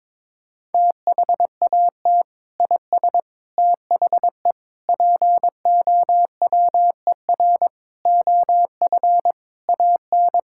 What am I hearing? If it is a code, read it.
Morse code, heard as a keyed tone, about 22 wpm, 700 Hz. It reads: THAT IS THE POWER OF AN
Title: Sentences from Top 1000 Words - Review Part 5 - 22wpm